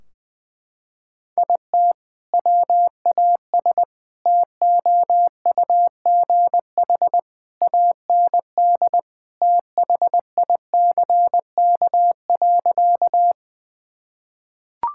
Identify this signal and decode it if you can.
Morse code, heard as a keyed tone, about 20 wpm, 700 Hz. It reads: IT WAS TOUGH AND THICK.
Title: Sentences from Top 1600 Words 20wpm